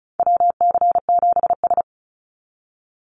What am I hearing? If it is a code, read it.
Morse code, heard as a keyed tone, about 34 wpm, 700 Hz. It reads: WC7S
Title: Call Signs - 35wpm